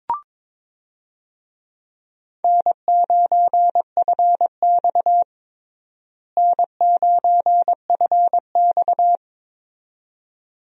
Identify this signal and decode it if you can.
Morse code, heard as a keyed tone, about 22 wpm, 700 Hz. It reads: N9FX N9FX
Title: USA Call Signs - 4 Characters - Repeated 2x - 22wpm